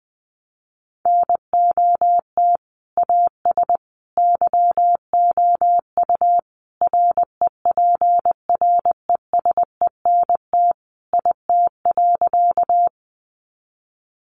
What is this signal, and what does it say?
Morse code, heard as a keyed tone, about 19 wpm, 700 Hz. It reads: NOT AS YOU REPRESENT IT.
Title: Sentences from Top 1200 Words 20wpm